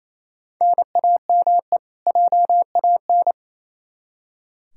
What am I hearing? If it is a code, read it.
Morse code, heard as a keyed tone, about 28 wpm, 700 Hz. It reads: NAME JAN
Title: Practice QSO - Exchange 1 - Beginning - 28wpm